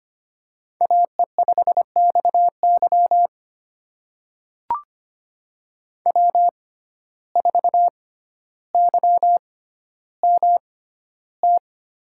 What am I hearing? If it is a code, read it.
Morse code, heard as a keyed tone, about 25 wpm, spaced as for 13 wpm, 700 Hz. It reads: AE5XY W4YMT
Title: USA Call Signs - 5 Characters - 6x Character Spacing - 25wpm